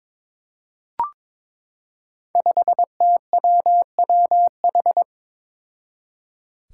Morse code, heard as a keyed tone, about 22 wpm, 700 Hz. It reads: 5TWWH
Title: Five Challenging Letter-Number Combinations 22wpm